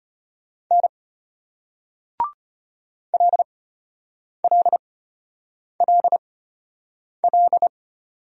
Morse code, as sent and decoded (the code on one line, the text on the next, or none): -. / .-.. .-.. .-.. .-..
N LLLL